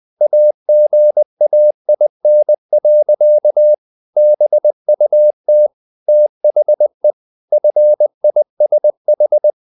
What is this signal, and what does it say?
Morse code, heard as a keyed tone, about 20 wpm, 600 Hz. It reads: AGAIN. BUT THE FISH
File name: the old man and the sea / the old man and the sea part 1 20wpm